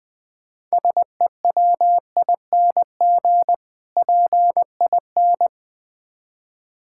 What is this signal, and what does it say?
Morse code, heard as a keyed tone, about 22 wpm, 700 Hz. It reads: SEWING PIN
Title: Difficult Words - Speed-Racing with Hints - 20wpm